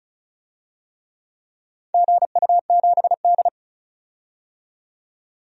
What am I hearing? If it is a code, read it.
Morse code, heard as a keyed tone, about 34 wpm, 700 Hz. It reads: GU7D
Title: DXCC-100 Call Signs - 4 Characters - 35wpm